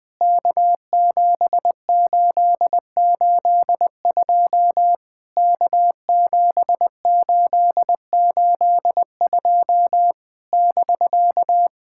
Morse code, heard as a keyed tone, about 20 wpm, 700 Hz. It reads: K7882 K7882 <BK>
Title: POTA QSO Practice - 20wpm